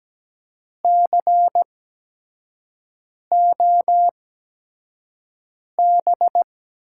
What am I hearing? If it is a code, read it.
Morse code, heard as a keyed tone, about 17 wpm, spaced as for 5 wpm, 700 Hz. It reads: COB
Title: USA Call Signs - 6 Characters - 8x Character Spacing - 17wpm